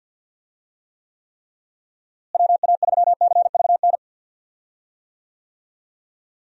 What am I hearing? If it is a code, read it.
Morse code, heard as a keyed tone, about 50 wpm, 700 Hz. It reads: WA3CVN
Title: USA Call Signs - 6 Characters - Repeated 3x - 50wpm